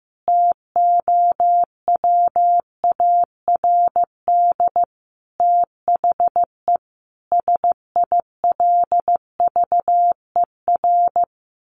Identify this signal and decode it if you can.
Morse code, heard as a keyed tone, about 15 wpm, 700 Hz. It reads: TOWARD THE SILVER
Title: A Sound Of Thunder 15wpm